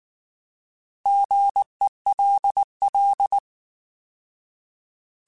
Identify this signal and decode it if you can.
Morse code, heard as a keyed tone, about 19 wpm, 800 Hz. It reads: GELL